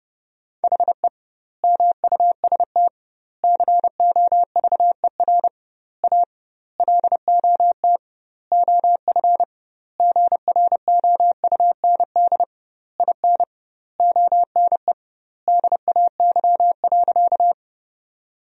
Morse code, heard as a keyed tone, about 30 wpm, 700 Hz. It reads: HE MUST COVER A LOT OF GROUND IN ONE DAY.
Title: 2x Word-Spacing - Sentences from Top 500 Words - 30wpm